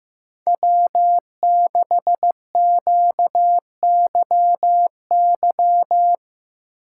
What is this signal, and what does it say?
Morse code, heard as a keyed tone, about 15 wpm, 700 Hz. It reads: W6QYY
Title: USA Call Signs - 5 Characters - 8x Character Spacing - 15wpm